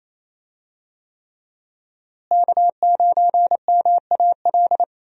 Morse code, heard as a keyed tone, about 28 wpm, 700 Hz. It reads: K9MAL
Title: USA Call Signs - 5 Characters - Repeated 3x - 28wpm